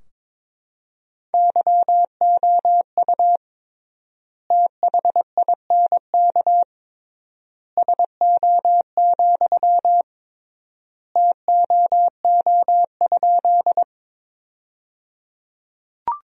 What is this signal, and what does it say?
Morse code, heard as a keyed tone, about 22 wpm, 700 Hz. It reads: YOU THINK SO, TOO?
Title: 3x Word-Spacing - Sentences from Top 200 Words - 22wpm